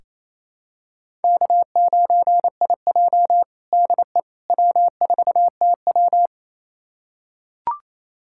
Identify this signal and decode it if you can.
Morse code, heard as a keyed tone, about 28 wpm, 700 Hz. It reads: K9IJ DE W4TW
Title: USA Callsign DE Callsign - 4 Characters - 28wpm